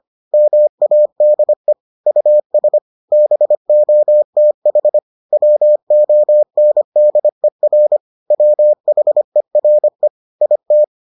Morse code, heard as a keyed tone, about 25 wpm, 600 Hz. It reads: MADE US BOTH WONDER WHERE IT